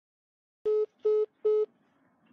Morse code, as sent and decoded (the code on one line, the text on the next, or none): none